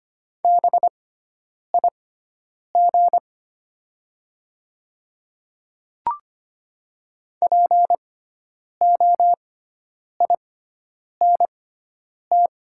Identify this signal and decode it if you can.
Morse code, heard as a keyed tone, about 25 wpm, spaced as for 9 wpm, 700 Hz. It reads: BIG POINT
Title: 6x Character Spacing - Top 300 Words - 25wpm